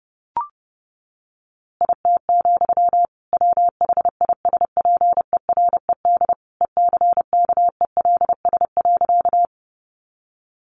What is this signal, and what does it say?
Morse code, heard as a keyed tone, about 30 wpm, 700 Hz. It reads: IT, WHISPERED ECKELS.
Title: A Sound Of Thunder 30wpm